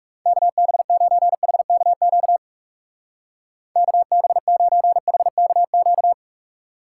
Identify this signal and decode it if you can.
Morse code, heard as a keyed tone, about 35 wpm, 700 Hz. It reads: KB9HKQ KB9HKQ
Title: Speed Racing - USA 2x3 Call Signs 30wpm